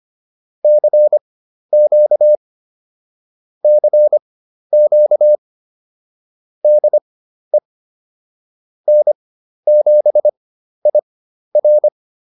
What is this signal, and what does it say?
Morse code, heard as a keyed tone, about 25 wpm, spaced as for 12 wpm, 600 Hz. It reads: CQ CQ DE N7IR